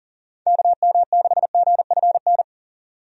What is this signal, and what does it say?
Morse code, heard as a keyed tone, about 40 wpm, 700 Hz. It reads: KM6GFN
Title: USA Call Signs - 6 Characters - 40wpm